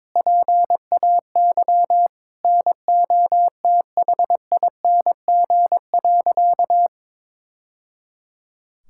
Morse code, heard as a keyed tone, about 22 wpm, 700 Hz. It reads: PAY NOTHING.
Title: Sentences from Top 1000 Words - Review Part 5 - 22wpm